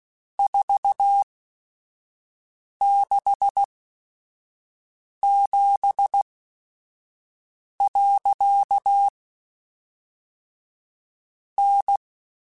Morse code, heard as a keyed tone, about 16 wpm, spaced as for 5 wpm, 800 Hz. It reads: 467. N